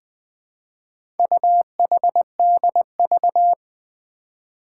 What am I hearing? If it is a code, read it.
Morse code, heard as a keyed tone, about 20 wpm, 700 Hz. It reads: UHDV
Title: Four Challenging Letter-Number Combinations 20wpm